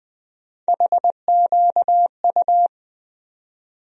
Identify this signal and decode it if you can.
Morse code, heard as a keyed tone, about 21 wpm, 700 Hz. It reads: HQU
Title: USA Call Sign Postfixes - 3 Letters 20wpm